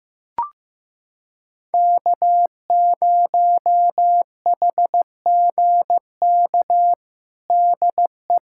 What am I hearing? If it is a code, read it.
Morse code, heard as a keyed tone, about 15 wpm, 700 Hz. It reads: K0HGK DE
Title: USA Callsign DE Callsign - 5 Characters - 15wpm